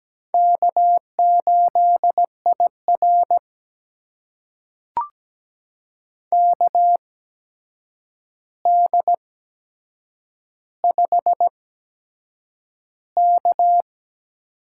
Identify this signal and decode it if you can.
Morse code, heard as a keyed tone, about 17 wpm, spaced as for 7 wpm, 700 Hz. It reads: K8IR KD5K
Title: USA Call Signs - 4 Characters - 8x Character Spacing - 17wpm